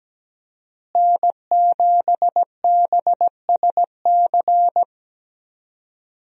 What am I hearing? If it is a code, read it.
Morse code, heard as a keyed tone, about 17 wpm, 700 Hz. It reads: N7BSC
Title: USA Call Signs - 5 Characters - Repeated 2x - 17wpm